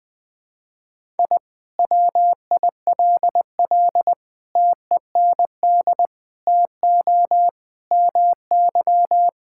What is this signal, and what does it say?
Morse code, heard as a keyed tone, about 20 wpm, 700 Hz. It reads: I WILL TEND TO MY
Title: Sentences from Top 1000 Words - Review Part 5 - 20wpm